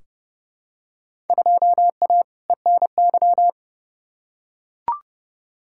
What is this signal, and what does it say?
Morse code, heard as a keyed tone, about 32 wpm, 700 Hz. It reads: 2A ENY
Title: Field Day Exchanges - 30wpm